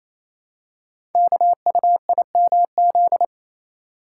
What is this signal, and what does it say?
Morse code, heard as a keyed tone, about 29 wpm, 700 Hz. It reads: KUIMZ
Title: Code Groups - 5 Letters - 28wpm